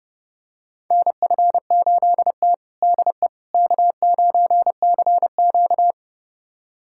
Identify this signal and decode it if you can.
Morse code, heard as a keyed tone, about 32 wpm, 700 Hz. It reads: NF8T DE K9CQ
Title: USA Callsign DE Callsign - 4 Characters - 30wpm